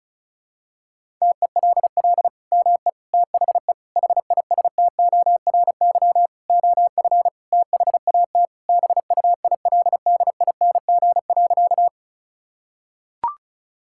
Morse code, heard as a keyed tone, about 34 wpm, 700 Hz. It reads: TELL ME THE HISTORY OF THAT BUILDING.
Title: Sentences from Top 1000 Words - Review Part 5 - 35wpm